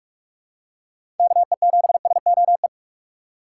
Morse code, heard as a keyed tone, about 46 wpm, 700 Hz. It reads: KE7SOE
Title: USA Call Signs - 6 Characters - 6x Character Spacing - 45wpm